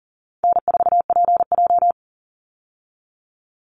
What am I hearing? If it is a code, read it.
Morse code, heard as a keyed tone, about 40 wpm, 700 Hz. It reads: N4PJ
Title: Easy Call Signs Repeated 2x - 40wpm